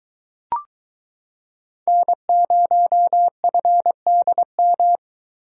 Morse code, heard as a keyed tone, about 23 wpm, 700 Hz. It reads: N0FDM